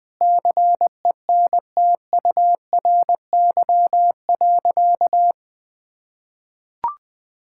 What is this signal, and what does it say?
Morse code, heard as a keyed tone, about 21 wpm, 700 Hz. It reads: CENTURY.
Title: Sentences with 5-Word Phrases - 20wpm